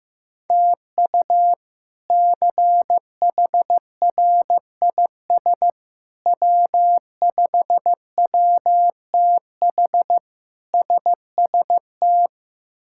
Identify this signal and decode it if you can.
Morse code, heard as a keyed tone, about 16 wpm, 700 Hz. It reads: TU CHRIS W5WTH SST
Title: K1USN SST QSO Practice - 15wpm